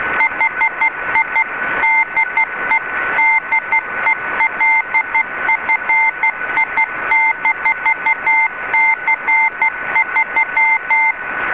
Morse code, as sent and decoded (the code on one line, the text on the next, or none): .... .. / -.. / -.. .-.. ..-. .. -....- -.-. ...--
HI DE DELFI-C3